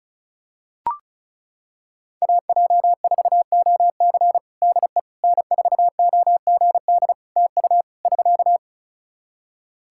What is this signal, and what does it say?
Morse code, heard as a keyed tone, about 36 wpm, 700 Hz. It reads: AJ4OC DE N4OGD TU <SK>
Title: Practice QSO - Exchange 4 - Ending - 35wpm